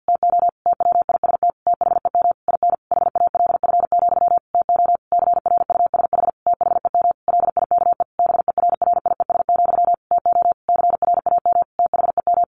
Morse code, heard as a keyed tone, about 50 wpm, 700 Hz. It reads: TO TWIST THEM IN HALF, TO CRUSH THEM LIKE BERRIES, TO CRAM THEM
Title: A Sound Of Thunder 50wpm